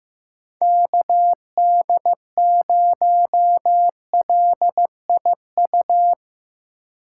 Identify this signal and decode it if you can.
Morse code, heard as a keyed tone, about 16 wpm, 700 Hz. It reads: KD0LIU
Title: Speed Racing - USA 2x3 Call Signs 15wpm